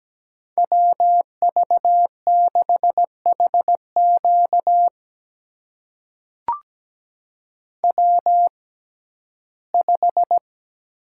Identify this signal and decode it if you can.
Morse code, heard as a keyed tone, about 17 wpm, spaced as for 12 wpm, 700 Hz. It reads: WV6HQ W5
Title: USA Call Signs - 5 Characters - 6x Character Spacing - 17wpm